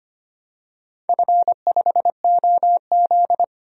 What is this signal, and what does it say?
Morse code, heard as a keyed tone, about 25 wpm, 700 Hz. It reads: F5OZ
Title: DXCC-100 Call Signs - 4 Characters - 25wpm